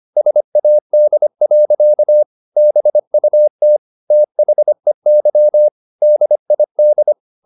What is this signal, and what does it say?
Morse code, heard as a keyed tone, about 25 wpm, 600 Hz. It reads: SAD. BUT THEY DID
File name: the old man and the sea part 1 25wpm